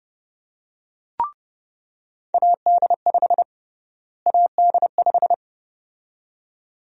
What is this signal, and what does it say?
Morse code, heard as a keyed tone, about 32 wpm, 700 Hz. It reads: AD5 AD5
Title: USA Call Sign Prefixes - Repeated 2x - 30wpm